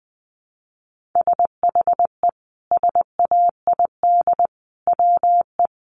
Morse code, heard as a keyed tone, about 20 wpm, 700 Hz. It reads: SHE SAID WE
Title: Sentences from Top 1300 Words 20wpm